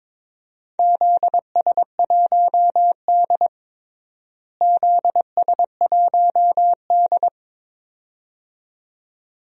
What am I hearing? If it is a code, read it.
Morse code, heard as a keyed tone, about 22 wpm, 700 Hz. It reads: ZS1D ZS1D
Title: DXCC-100 Call Signs - 4 Characters - Repeated 2x - 22wpm